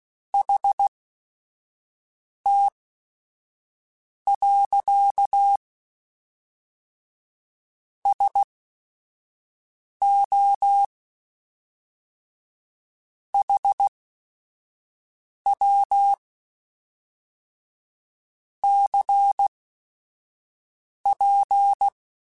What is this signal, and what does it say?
Morse code, heard as a keyed tone, about 16 wpm, spaced as for 5 wpm, 800 Hz. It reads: HT. SO HW CP
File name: QSOs / QSOs-mp3s / WA7QOX 1@5